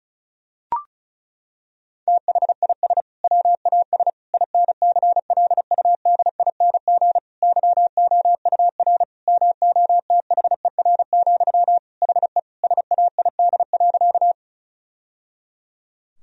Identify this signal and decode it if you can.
Morse code, heard as a keyed tone, about 36 wpm, 700 Hz. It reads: THIS WAS INCLUDING YOUR MOTHER, HE SAID.
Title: Sentences from Top 1000 Words - Review Part 4 - 35wpm